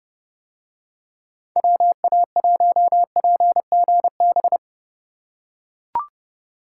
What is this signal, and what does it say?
Morse code, heard as a keyed tone, about 30 wpm, 700 Hz. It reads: WA1PGB